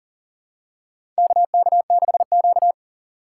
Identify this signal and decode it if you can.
Morse code, heard as a keyed tone, about 40 wpm, 700 Hz. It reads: KK6Q
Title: USA Call Signs - 4 Characters - 4x Character Spacing - 40wpm